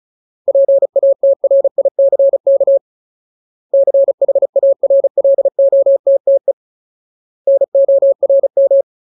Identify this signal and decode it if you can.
Morse code, heard as a keyed tone, about 34 wpm, 550 Hz. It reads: PATRICK CHARLOTTE NORM
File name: Ultimate Word List / 35WPM Ultimate